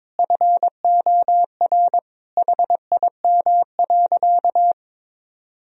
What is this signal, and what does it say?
Morse code, heard as a keyed tone, about 22 wpm, 700 Hz. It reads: FOR HIM.
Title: Sentences from Top 1000 Words - Review Part 4 - 22wpm